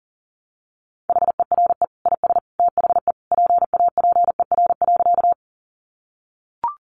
Morse code, heard as a keyed tone, about 40 wpm, 700 Hz. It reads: HERE IS THE PAPER.
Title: Sentences from Top 300 Words 40wpm